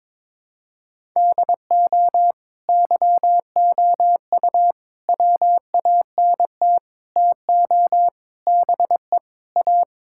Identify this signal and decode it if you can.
Morse code, heard as a keyed tone, about 22 wpm, 700 Hz. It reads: DO YOU WANT TO BE A
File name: practice / 22wpm / Sentences with 4-Word Phrases - 22wpm